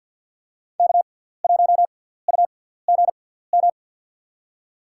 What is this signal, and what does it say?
Morse code, heard as a keyed tone, about 53 wpm, spaced as for 17 wpm, 690 Hz. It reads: K1UGM
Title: USA Call Signs - 5 Characters - 6x Character Spacing - 50wpm